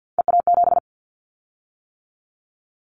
Morse code, heard as a keyed tone, about 50 wpm, 700 Hz. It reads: EA7
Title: Callsign Prefixes for Top 25 DXCC - 50wpm